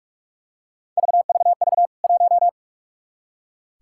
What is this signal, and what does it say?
Morse code, heard as a keyed tone, about 46 wpm, 700 Hz. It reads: VVV 1
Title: K1USN SST QSO Practice 45wpm